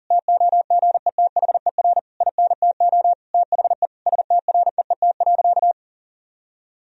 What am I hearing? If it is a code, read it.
Morse code, heard as a keyed tone, about 40 wpm, 700 Hz. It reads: TOGETHER INTO THE STREET.
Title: Sentences from Top 1000 Words - Review Part 3 - 40wpm